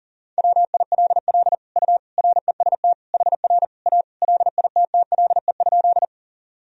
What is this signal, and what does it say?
Morse code, heard as a keyed tone, about 39 wpm, 700 Hz. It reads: WILL U REST HR A LITTLE?
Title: Sentences from Top 500 Words and Abbreviations 40wpm